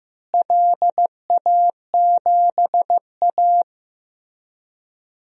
Morse code, heard as a keyed tone, about 15 wpm, 700 Hz. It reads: LA7A
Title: DXCC-100 Call Signs - 4 Characters - Repeated 3x - 15wpm